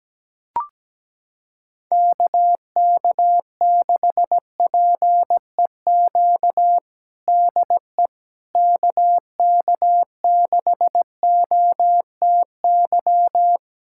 Morse code, heard as a keyed tone, about 17 wpm, 700 Hz. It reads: KK6PEQ DE KK6OTY
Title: USA Callsign DE Callsign - 6 Characters - 17wpm